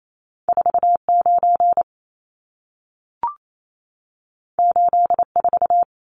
Morse code, heard as a keyed tone, about 28 wpm, 700 Hz. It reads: 49 84
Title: Numbers - 2-Digit 28wpm